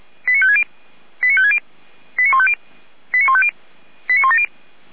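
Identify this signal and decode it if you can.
Five-tone selective call (ZVEI), heard as quick runs of five tones, tones of about 80 ms. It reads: 89560 89560 89150 89150 89170
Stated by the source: ZVEI five-tone selective call received by SDR, fire brigade Orlamünde (feuerwehr-orlamuende).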